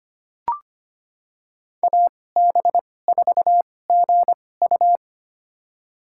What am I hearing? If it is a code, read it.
Morse code, heard as a keyed tone, about 25 wpm, spaced as for 18 wpm, 700 Hz. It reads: AB4GU
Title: USA Call Signs - 5 Characters - 2x Character Spacing - 25wpm